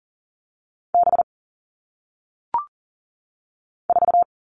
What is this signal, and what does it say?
Morse code, heard as a keyed tone, about 40 wpm, 700 Hz. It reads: B 4
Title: All Cut Numbers - 1-Digit 40wpm